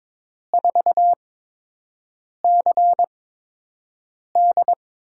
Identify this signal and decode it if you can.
Morse code, heard as a keyed tone, about 22 wpm, spaced as for 6 wpm, 700 Hz. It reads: 4CD
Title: USA Call Signs - 5 Characters - 8x Character Spacing - 22wpm